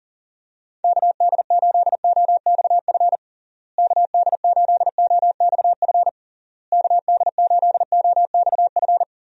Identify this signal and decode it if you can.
Morse code, heard as a keyed tone, about 40 wpm, 700 Hz. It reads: KD8OXF KD8OXF KD8OXF
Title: USA Call Signs - 6 Characters - Repeated 3x - 40wpm